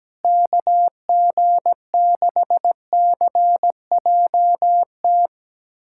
Morse code, heard as a keyed tone, about 17 wpm, 700 Hz. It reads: KG6CJT
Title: USA Callsign DE Callsign 17wpm